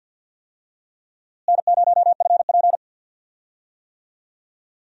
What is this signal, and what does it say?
Morse code, heard as a keyed tone, about 47 wpm, 700 Hz. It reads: N0FP
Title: USA Call Signs - 4 Characters - Repeated 3x - 50wpm